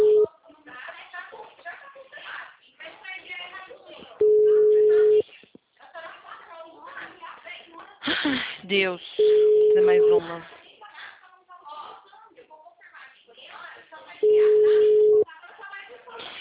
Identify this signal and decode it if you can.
Ringback tone (Europe, 425 Hz), with speech.